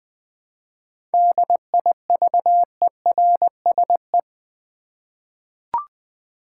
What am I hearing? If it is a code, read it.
Morse code, heard as a keyed tone, about 20 wpm, 700 Hz. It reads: DIVERSE
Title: 7 Letter Words - First 3 Letters Given - 20wpm